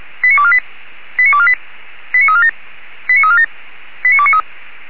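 Five-tone selective call (ZVEI), heard as quick runs of five tones, tones of about 70 ms. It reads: 89248 89248 89347 89347 89293
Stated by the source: ZVEI five-tone selective call received on SDR.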